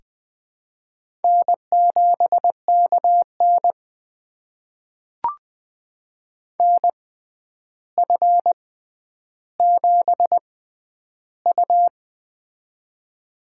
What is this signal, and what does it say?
Morse code, heard as a keyed tone, about 21 wpm, spaced as for 10 wpm, 700 Hz. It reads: N7KN NF7U
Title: USA Call Signs - 4 Characters - 6x Character Spacing - 20wpm